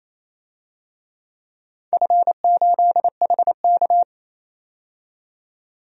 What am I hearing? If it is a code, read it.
Morse code, heard as a keyed tone, about 28 wpm, 700 Hz. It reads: F8HK
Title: DXCC-100 Call Signs - 4 Characters - 28wpm